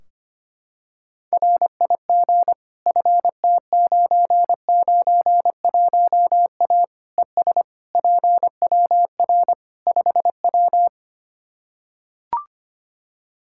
Morse code, heard as a keyed tone, about 25 wpm, 700 Hz. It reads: RIG FT991A ES PWR 5W
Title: Practice QSO - Exchange 2 - Info - 25wpm